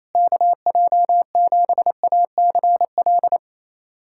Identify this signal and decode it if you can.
Morse code, heard as a keyed tone, about 28 wpm, 700 Hz. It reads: KJ7ACL
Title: USA Callsign DE Callsign 28wpm